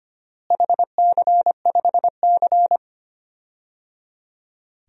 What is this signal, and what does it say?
Morse code, heard as a keyed tone, about 25 wpm, 700 Hz. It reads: HC5C
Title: DXCC-100 Call Signs - 4 Characters - Repeated 3x - 25wpm